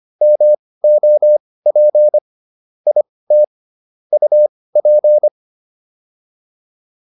Morse code, heard as a keyed tone, about 25 wpm, spaced as for 18 wpm, 600 Hz. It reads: MOP IT UP